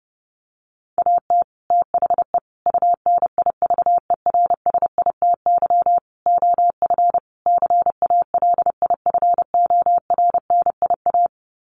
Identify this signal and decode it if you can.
Morse code, heard as a keyed tone, about 30 wpm, 700 Hz. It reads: AT THE UNIVERSITY OF CALIFORNIA
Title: Sets of 5 Words Encore2 30wpm